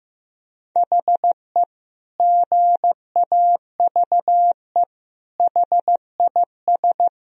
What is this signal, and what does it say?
Morse code, heard as a keyed tone, about 15 wpm, 700 Hz. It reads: HE GAVE HIS
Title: Sentences from Top 1000 Words - Review Part 1 - 15wpm